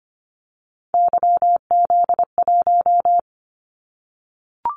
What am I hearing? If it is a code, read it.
Morse code, heard as a keyed tone, about 25 wpm, 700 Hz. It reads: YZ1
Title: Callsign Prefixes for Top 25 DXCC - 25wpm